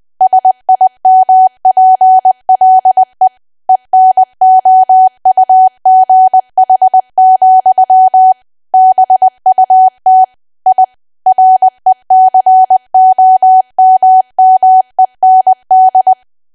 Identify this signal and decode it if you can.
Morse code, heard as a keyed tone, about 20 wpm, 750 Hz. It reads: SIMPLE ENOUGH, BUT I RECOMMEND
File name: Archive / 20 WPM / 20WPM